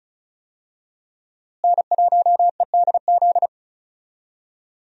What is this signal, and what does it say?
Morse code, heard as a keyed tone, about 36 wpm, 700 Hz. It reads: N1EDZ
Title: USA Call Signs - 5 Characters - Repeated 3x - 35wpm